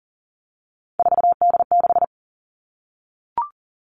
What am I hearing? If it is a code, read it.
Morse code, heard as a keyed tone, about 40 wpm, 700 Hz. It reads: 4D6
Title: All Cut Numbers - 3-Digit 40wpm